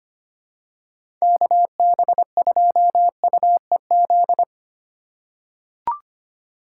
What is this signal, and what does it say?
Morse code, heard as a keyed tone, about 25 wpm, 700 Hz. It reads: KB2UEZ
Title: USA Call Signs - 8x Character Spacing - 25wpm